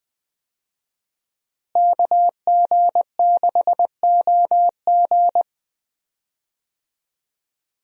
Morse code, heard as a keyed tone, about 21 wpm, 700 Hz. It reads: KG6OG
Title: USA Call Signs - 5 Characters - Repeated 2x - 20wpm